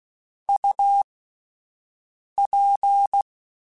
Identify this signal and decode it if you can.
Morse code, heard as a keyed tone, about 16 wpm, spaced as for 5 wpm, 800 Hz. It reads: UP
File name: Spaced Characters / Spaced Characters-mp3s / Weather5 5 sp @10